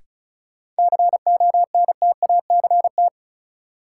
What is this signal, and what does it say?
Morse code, heard as a keyed tone, about 34 wpm, 700 Hz. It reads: CONTACT
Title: Sentences from Top 1000 Words - Review Part 7 - New Word Drill - 35wpm